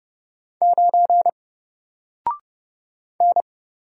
Morse code, heard as a keyed tone, about 30 wpm, 700 Hz. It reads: 9 N
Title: Single Letter-Number Alt 30wpm